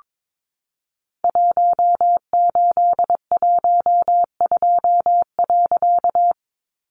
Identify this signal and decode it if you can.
Morse code, heard as a keyed tone, about 22 wpm, 700 Hz. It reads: 1812.
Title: A Sound Of Thunder 22wpm